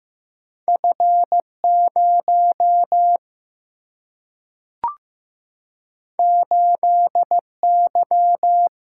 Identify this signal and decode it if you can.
Morse code, heard as a keyed tone, about 15 wpm, 700 Hz. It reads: F 0 8 Y
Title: Single-Letter Single-Number 15wpm